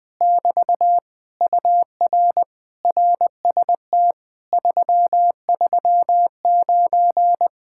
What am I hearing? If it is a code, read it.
Morse code, heard as a keyed tone, about 20 wpm, 700 Hz. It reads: <BT> UR RST 339
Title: Speed-Racing Common QSO Phrases Encore - 20wpm